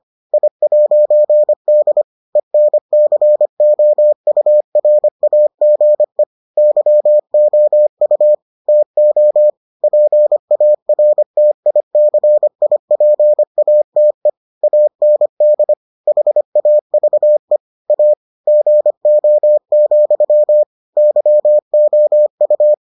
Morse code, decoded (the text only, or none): I'D ENCOURAGE YOU TO PARTICIPATE AND HAVE A GO, YOU